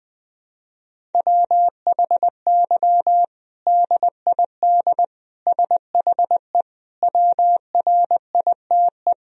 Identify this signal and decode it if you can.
Morse code, heard as a keyed tone, about 20 wpm, 700 Hz. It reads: WHY DID SHE WRITE
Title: Sentences from Top 1000 Words - Review Part 1 - 20wpm